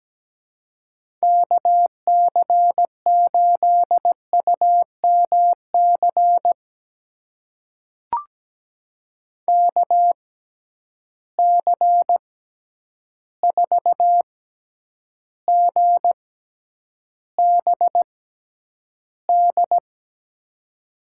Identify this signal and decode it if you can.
Morse code, heard as a keyed tone, about 17 wpm, spaced as for 9 wpm, 700 Hz. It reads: KC8UMC KC4GBD
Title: USA Call Signs - 6 Characters - 6x Character Spacing - 17wpm